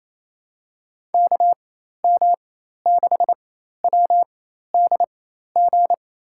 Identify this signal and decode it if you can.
Morse code, heard as a keyed tone, about 28 wpm, spaced as for 13 wpm, 700 Hz. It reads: KM6WDG
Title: USA Call Signs - 6 Characters - 4x Character Spacing - 28wpm